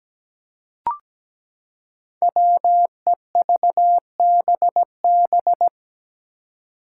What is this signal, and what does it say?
Morse code, heard as a keyed tone, about 17 wpm, 700 Hz. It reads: WEVBB
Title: Five Challenging Letter-Number Combinations 17wpm